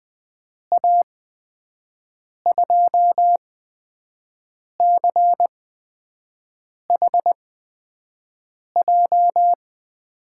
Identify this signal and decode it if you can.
Morse code, heard as a keyed tone, about 20 wpm, spaced as for 5 wpm, 700 Hz. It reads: A2CHJ
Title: USA Call Signs - 8x Character Spacing - 20wpm